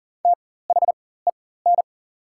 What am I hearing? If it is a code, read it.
Morse code, heard as a keyed tone, about 40 wpm, spaced as for 19 wpm, 690 Hz. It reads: THEN
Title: 4x Character Spacing - Top 500 Words - 40wpm